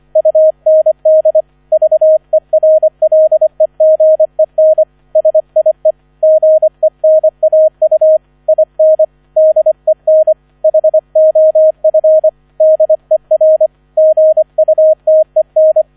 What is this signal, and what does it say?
Morse code, heard as a keyed tone, about 25 wpm, 630 Hz, with radio noise. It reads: UND VERLEGEN SIE GENAU IN DEN HOF DER GUTEN